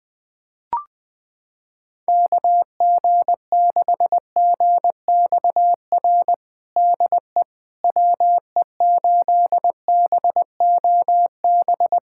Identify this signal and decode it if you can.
Morse code, heard as a keyed tone, about 20 wpm, 700 Hz. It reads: KG6GXR DE WE8BOB
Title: USA Callsign DE Callsign 20wpm